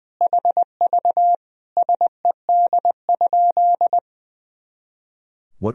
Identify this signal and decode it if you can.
Morse code, heard as a keyed tone, about 20 wpm, 700 Hz. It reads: HV SED?
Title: Sentences from Top 100 Words and Abbreviations 20wpm